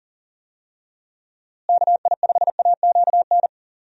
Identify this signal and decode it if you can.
Morse code, heard as a keyed tone, about 38 wpm, 700 Hz. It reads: KI5AQN
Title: USA Call Signs - 6 Characters - Repeated 2x - 40wpm